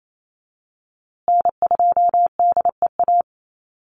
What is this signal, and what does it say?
Morse code, heard as a keyed tone, about 28 wpm, 700 Hz. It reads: N2DEA
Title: USA Call Signs Repeated 3x 28wpm